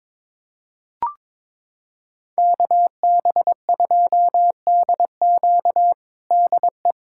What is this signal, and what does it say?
Morse code, heard as a keyed tone, about 21 wpm, 700 Hz. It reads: KB2DQ DE
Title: USA Callsign DE Callsign 22wpm